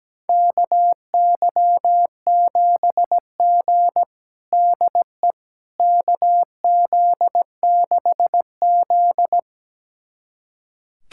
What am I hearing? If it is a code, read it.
Morse code, heard as a keyed tone, about 17 wpm, 700 Hz. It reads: KY7G DE KZ6Z
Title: USA Callsign DE Callsign - 4 Characters - 17wpm